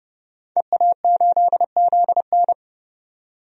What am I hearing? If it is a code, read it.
Morse code, heard as a keyed tone, about 31 wpm, 700 Hz. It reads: EA8ZN
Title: DXCC-100 Call Signs - 5 Characters - 30wpm